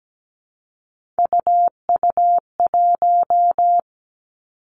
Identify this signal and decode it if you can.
Morse code, heard as a keyed tone, about 17 wpm, 700 Hz. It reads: UU1
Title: Three Challenging Letter-Number Combinations 17wpm